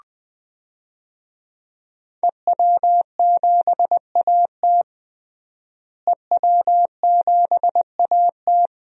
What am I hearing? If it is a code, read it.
Morse code, heard as a keyed tone, about 20 wpm, 700 Hz. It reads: EW7AT EW7AT
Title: DXCC-100 Call Signs - 5 Characters - Repeated 3x - 20wpm